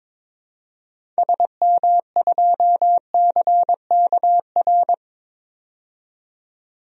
Morse code, heard as a keyed tone, about 22 wpm, 700 Hz. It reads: SM2CKR
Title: DXCC-100 Call Signs - 6 Characters - 22wpm